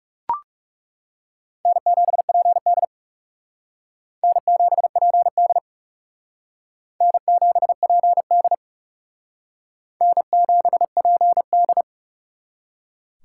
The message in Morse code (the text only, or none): N7PD N7PD N7PD N7PD